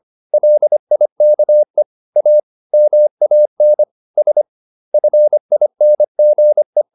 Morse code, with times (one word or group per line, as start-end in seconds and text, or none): 0.3-1.8 LIKE
2.2-2.4 A
2.7-3.8 MAN
4.2-4.4 S
4.9-6.8 FINGE